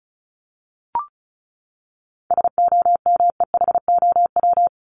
Morse code, heard as a keyed tone, about 34 wpm, 700 Hz. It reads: SOMEHOW
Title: Top 1501-1750 Words 35wpm